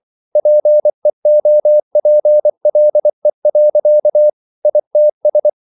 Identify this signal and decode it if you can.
Morse code, heard as a keyed tone, about 24 wpm, 600 Hz. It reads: PEOPLE. ITS